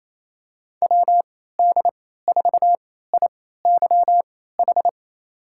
Morse code, heard as a keyed tone, about 28 wpm, spaced as for 16 wpm, 700 Hz. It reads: WD4IYH
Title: USA Call Signs - 6 Characters - 3x Character Spacing - 28wpm